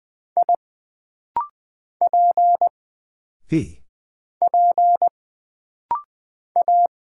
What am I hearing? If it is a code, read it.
Morse code, heard as a keyed tone, about 20 wpm, 700 Hz. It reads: I P P A
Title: Single Character (Letter:Number:Punctuation:Prosign) - Rapid-Fire - 20wpm